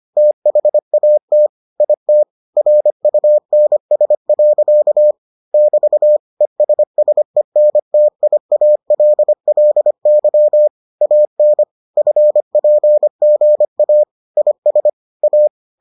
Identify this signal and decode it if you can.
Morse code, heard as a keyed tone, about 25 wpm, 600 Hz. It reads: THAT IT RUNS. <BT> ESSENTIALLY AN FPGA IS A